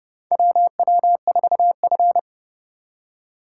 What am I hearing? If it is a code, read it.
Morse code, heard as a keyed tone, about 30 wpm, 700 Hz. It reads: WW4F
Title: USA Call Signs - 4 Characters - 2x Character Spacing - 30wpm